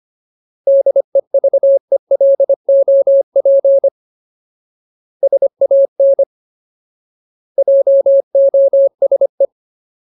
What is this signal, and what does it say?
Morse code, heard as a keyed tone, about 25 wpm, 550 Hz. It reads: DEVELOP SAN JOSE
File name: Ultimate Word List / 25WPM Ultimate